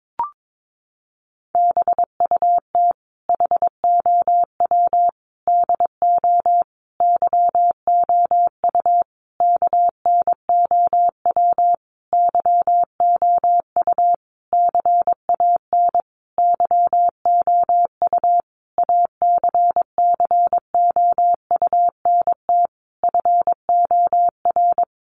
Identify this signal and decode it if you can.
Morse code, heard as a keyed tone, about 22 wpm, 700 Hz. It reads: BUT HOW DO YOU KNOW YOU CAN YOU ACCOUNT FOR
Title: Sentences from Top 900 Words 22wpm